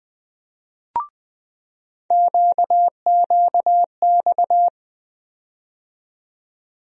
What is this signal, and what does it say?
Morse code, heard as a keyed tone, about 20 wpm, 700 Hz. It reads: QQX
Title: USA Call Sign Postfixes - 3 Letters 20wpm